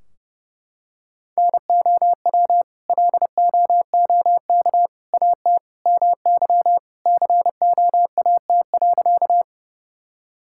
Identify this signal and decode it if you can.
Morse code, heard as a keyed tone, about 29 wpm, 700 Hz. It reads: NOW LOOK AT MY COAT.
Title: Sentences from Top 1600 Words 30wpm